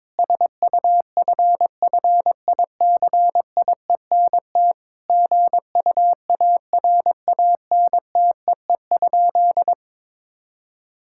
Morse code, heard as a keyed tone, about 22 wpm, 700 Hz. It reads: SUFFICIENT GUARANTEE?
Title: Sentences from Top 1600 Words 22wpm